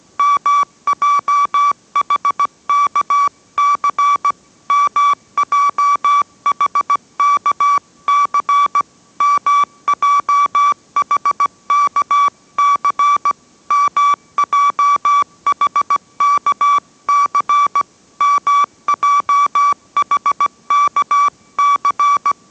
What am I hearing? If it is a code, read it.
Morse code, heard as a keyed tone, about 17 wpm, 1150 Hz. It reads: MJHKC MJHKC MJHKC MJHKC MJHKC